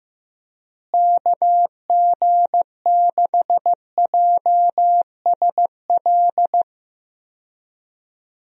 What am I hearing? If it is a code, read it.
Morse code, heard as a keyed tone, about 15 wpm, 700 Hz. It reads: KG6JSL
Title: USA Call Signs - 6 Characters - Repeated 3x - 15wpm